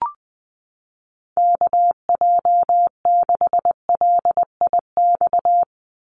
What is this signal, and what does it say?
Morse code, heard as a keyed tone, about 20 wpm, 700 Hz. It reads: KJ6LIX
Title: USA Call Signs 20wpm